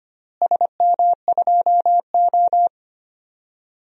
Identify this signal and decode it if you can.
Morse code, heard as a keyed tone, about 25 wpm, 700 Hz. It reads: SM2O
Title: DXCC-100 Call Signs - 4 Characters - Repeated 2x - 25wpm